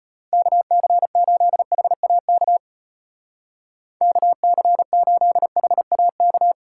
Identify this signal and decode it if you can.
Morse code, heard as a keyed tone, about 31 wpm, 700 Hz. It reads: KC8HAK KC8HAK
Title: Speed Racing - USA Call Signs 25wpm